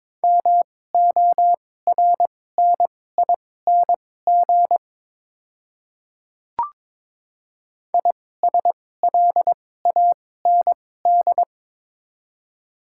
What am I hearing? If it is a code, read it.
Morse code, heard as a keyed tone, about 23 wpm, spaced as for 16 wpm, 700 Hz. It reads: MORNING ISLAND
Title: 2x Character Spacing - Top 500 Words - 22wpm